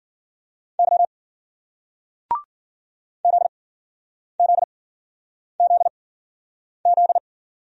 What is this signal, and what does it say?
Morse code, heard as a keyed tone, about 51 wpm, spaced as for 9 wpm, 700 Hz. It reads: Y ZZZZ